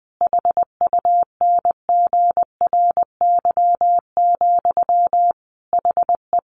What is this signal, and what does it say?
Morse code, heard as a keyed tone, about 20 wpm, 700 Hz. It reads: HUNGRY, HE
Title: Sentences from Top 1400 Words 20wpm